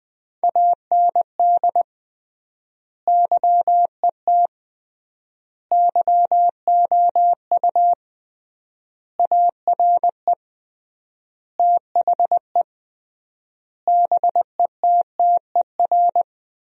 Morse code, heard as a keyed tone, about 21 wpm, 700 Hz. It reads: AND YET YOU ARE THE BETTER
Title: 3x Word-Spacing - Sentences from Top 500 Words - 20wpm